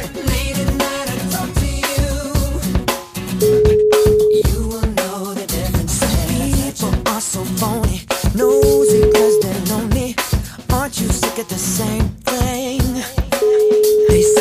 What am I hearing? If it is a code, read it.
Ringback tone (Europe, 425 Hz), with noise.